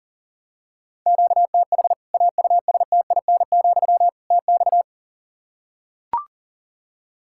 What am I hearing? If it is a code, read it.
Morse code, heard as a keyed tone, about 40 wpm, 700 Hz. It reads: QTH AUSTIN, TX
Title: Practice QSO - Exchange 1 - Beginning - 40wpm